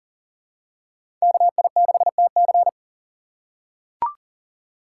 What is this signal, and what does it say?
Morse code, heard as a keyed tone, about 40 wpm, 700 Hz. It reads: KI6TC
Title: USA Call Signs - 2x Character Spacing - 40wpm